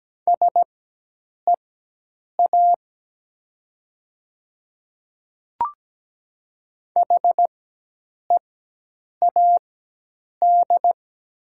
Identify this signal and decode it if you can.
Morse code, heard as a keyed tone, about 17 wpm, spaced as for 8 wpm, 700 Hz. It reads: SEA HEAD
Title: 4x Character Spacing - Top 500 Words - 17wpm